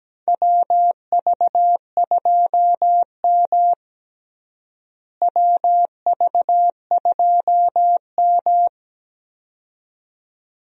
Morse code, heard as a keyed tone, about 17 wpm, 700 Hz. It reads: WV2M WV2M
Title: USA Call Signs - 4 Characters - Repeated 2x - 17wpm